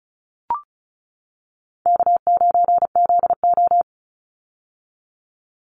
Morse code, heard as a keyed tone, about 34 wpm, 700 Hz. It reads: K9ZO
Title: Call Signs - 35wpm